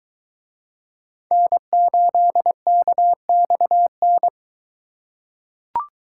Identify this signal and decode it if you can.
Morse code, heard as a keyed tone, about 20 wpm, 700 Hz. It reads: N8KXN